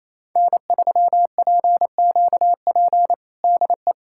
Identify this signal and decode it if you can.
Morse code, heard as a keyed tone, about 28 wpm, 700 Hz. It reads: N3PQP DE
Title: USA Callsign DE Callsign - 5 Characters - 28wpm